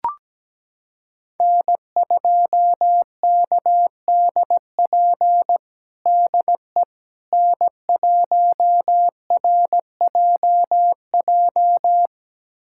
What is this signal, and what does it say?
Morse code, heard as a keyed tone, about 17 wpm, 700 Hz. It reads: N2KDP DE N1RJJ
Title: USA Callsign DE Callsign - 5 Characters - 17wpm